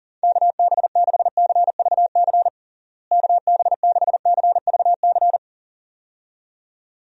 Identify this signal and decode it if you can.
Morse code, heard as a keyed tone, about 37 wpm, 700 Hz. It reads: KB6CVC KB6CVC
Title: USA Call Signs - 6 Characters - Repeated 2x - 40wpm